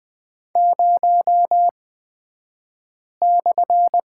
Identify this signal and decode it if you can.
Morse code, heard as a keyed tone, about 20 wpm, spaced as for 5 wpm, 700 Hz. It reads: 0/
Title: 254 Lesson 41d - Period - Two Characters 20wpm